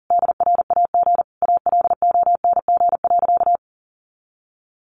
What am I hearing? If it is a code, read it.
Morse code, heard as a keyed tone, about 40 wpm, 700 Hz. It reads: DRAG ALONG.
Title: Sentences from Top 1500 Words 40wpm